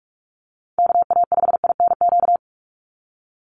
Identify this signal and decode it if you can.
Morse code, heard as a keyed tone, about 44 wpm, 700 Hz. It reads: KA5INQ
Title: USA Call Signs 45wpm